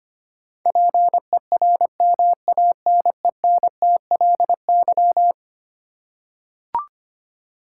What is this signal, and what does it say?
Morse code, heard as a keyed tone, about 25 wpm, 700 Hz. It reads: PERMANENTLY
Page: morse-code-ninja-practice-25wpm